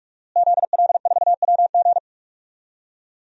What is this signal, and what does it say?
Morse code, heard as a keyed tone, about 45 wpm, 700 Hz. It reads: ZL4WG
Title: DXCC-100 Call Signs - 5 Characters - Repeated 3x - 45wpm